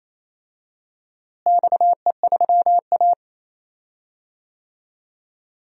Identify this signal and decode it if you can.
Morse code, heard as a keyed tone, about 28 wpm, 700 Hz. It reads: XE3A
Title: DXCC-100 Call Signs - 4 Characters - Repeated 2x - 28wpm